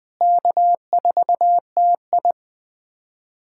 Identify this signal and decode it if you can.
Morse code, heard as a keyed tone, about 20 wpm, 700 Hz. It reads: K4TI
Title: USA Callsign DE Callsign 20wpm